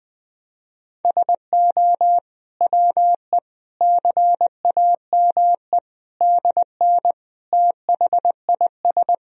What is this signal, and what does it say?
Morse code, heard as a keyed tone, about 20 wpm, 700 Hz. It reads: SO WE CAME DN THIS
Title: Sentences from Top 400 Words and Abbreviations 20wpm